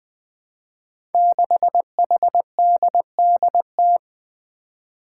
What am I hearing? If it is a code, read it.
Morse code, heard as a keyed tone, about 20 wpm, 700 Hz. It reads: 6HDDT
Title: Five Challenging Letter-Number Combinations 20wpm